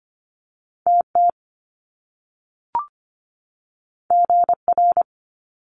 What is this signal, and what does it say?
Morse code, heard as a keyed tone, about 25 wpm, 700 Hz. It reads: TT GR
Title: Top Two Letter Combinations in Words 25wpm